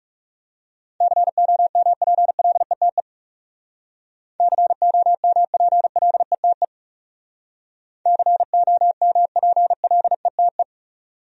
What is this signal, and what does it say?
Morse code, heard as a keyed tone, about 37 wpm, 700 Hz. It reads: COMPLETE COMPLETE COMPLETE
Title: Speed-Racing Top 500 Words 30wpm